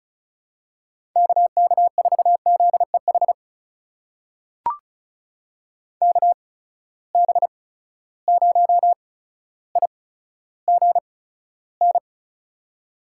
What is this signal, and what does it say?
Morse code, heard as a keyed tone, about 36 wpm, spaced as for 15 wpm, 700 Hz. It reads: KK4ZEH KB0IGN
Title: USA Call Signs - 6 Characters - 8x Character Spacing - 35wpm